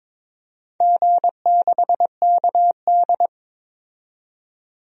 Morse code, heard as a keyed tone, about 22 wpm, 700 Hz. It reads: G6KD
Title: DXCC-100 Call Signs - 4 Characters - 22wpm